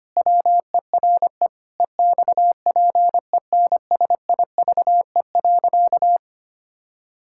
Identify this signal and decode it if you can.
Morse code, heard as a keyed tone, about 25 wpm, 700 Hz. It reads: WERE EXPENSIVE.
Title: Pangram Challenge v2 - 25wpm